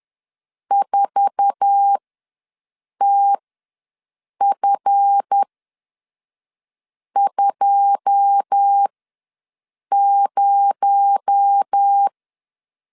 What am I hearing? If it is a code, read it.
Morse code, heard as a keyed tone, about 10 wpm, spaced as for 5 wpm, 800 Hz. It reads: 4TF20
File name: LEZIONI / 9 2 ws 10wpm 4130